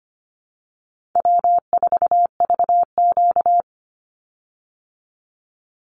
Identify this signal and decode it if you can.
Morse code, heard as a keyed tone, about 25 wpm, 700 Hz. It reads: W4VQ
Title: Easy Call Signs Repeated 3x - 25wpm